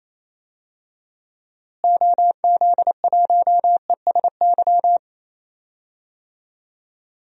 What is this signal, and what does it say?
Morse code, heard as a keyed tone, about 28 wpm, 700 Hz. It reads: OZ1ESY
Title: DXCC-100 Call Signs - 6 Characters - Repeated 2x - 28wpm